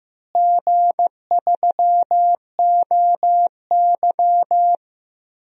Morse code, heard as a keyed tone, about 15 wpm, 700 Hz. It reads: G3OY
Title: DXCC-100 Call Signs - 4 Characters - Repeated 3x - 15wpm